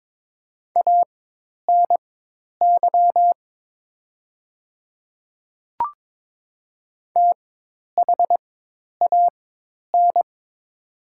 Morse code, heard as a keyed tone, about 22 wpm, spaced as for 10 wpm, 700 Hz. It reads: ANY THAN
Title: 4x Character Spacing - Top 200 Words - 22wpm